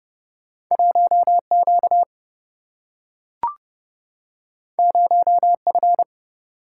Morse code, heard as a keyed tone, about 30 wpm, 700 Hz. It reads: 1 Q 0 F
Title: Single-Letter Single-Number 30wpm